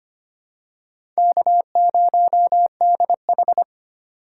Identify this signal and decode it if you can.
Morse code, heard as a keyed tone, about 25 wpm, 700 Hz. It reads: K0DH
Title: USA Call Signs - 2x Character Spacing - 25wpm